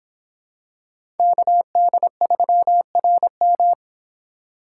Morse code, heard as a keyed tone, about 26 wpm, 700 Hz. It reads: KD3RM